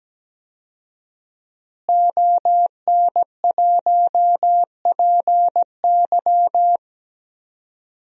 Morse code, heard as a keyed tone, about 17 wpm, 700 Hz. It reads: ON1PY
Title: DXCC-100 Call Signs - 5 Characters - Repeated 2x - 17wpm